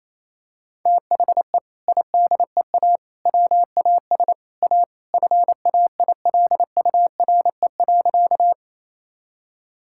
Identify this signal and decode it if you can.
Morse code, heard as a keyed tone, about 28 wpm, 700 Hz. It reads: THE IDEA WAS A FAILURE.
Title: Sentences from Top 1000 Words - Review Part 6 - 28wpm